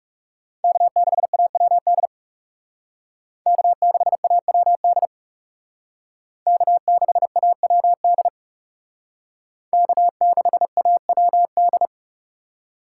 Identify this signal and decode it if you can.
Morse code, heard as a keyed tone, about 38 wpm, 700 Hz. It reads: K6AWD K6AWD K6AWD K6AWD